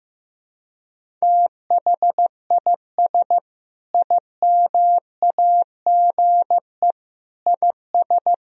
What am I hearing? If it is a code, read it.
Morse code, heard as a keyed tone, about 16 wpm, 700 Hz. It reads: THIS IMAGE IS
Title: Sentences from Top 1000 Words - Review Part 4 - 15wpm